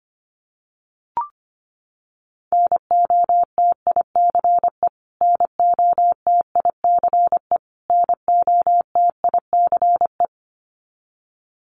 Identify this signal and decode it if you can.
Morse code, heard as a keyed tone, about 25 wpm, 700 Hz. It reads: NOTICE NOTICE NOTICE
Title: Top 500 Words Repeated 25wpm